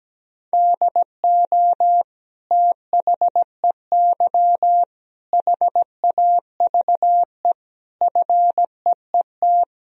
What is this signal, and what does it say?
Morse code, heard as a keyed tone, about 17 wpm, 700 Hz. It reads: DO THEY HAVE FEET
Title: Sentences from Top 1000 Words - Review Part 1 - 17wpm